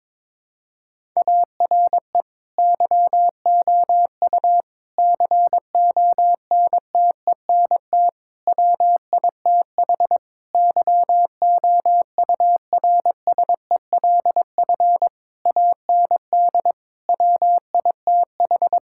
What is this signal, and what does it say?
Morse code, heard as a keyed tone, about 22 wpm, 700 Hz. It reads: ARE YOU CONTENT WITH YOURSELF AND WITH
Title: Sentences from Top 1000 Words - Review Part 6 - 22wpm